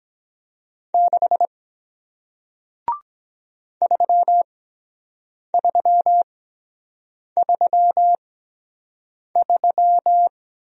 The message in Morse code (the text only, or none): 6 3333